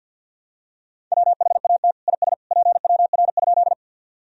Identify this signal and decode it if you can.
Morse code, heard as a keyed tone, about 51 wpm, 700 Hz. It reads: WHAT IS PWR?
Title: Sentences from Top 500 Words and Abbreviations 50wpm